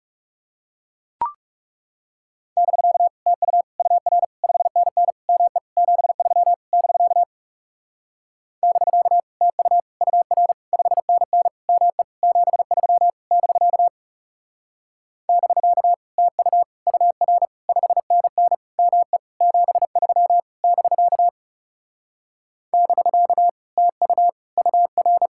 Morse code, heard as a keyed tone, about 38 wpm, 700 Hz. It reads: <BK> TU UR 5NN ME 73 <BK> <BK> TU UR 5NN ME 73 <BK> <BK> TU UR 5NN ME 73 <BK> <BK> TU UR
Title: Speed-Racing Common QSO Phrases - 30wpm